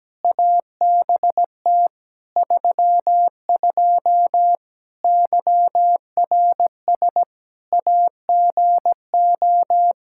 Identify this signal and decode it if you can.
Morse code, heard as a keyed tone, about 17 wpm, 700 Hz. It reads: ABT 32 YRS AGO
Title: Practice QSO - Exchange 3 - More Info - 17wpm